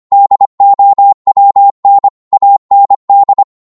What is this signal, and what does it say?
Morse code, heard as a keyed tone, about 25 wpm, 800 Hz. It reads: DOWN AND